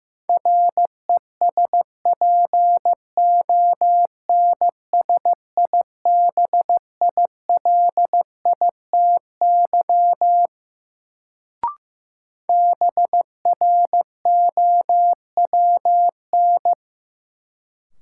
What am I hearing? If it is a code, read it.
Morse code, heard as a keyed tone, about 15 wpm, 700 Hz. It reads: RESPONSIBILITY BROWN
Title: Sentences from Top 1200 Words - New Word Drill - 15wpm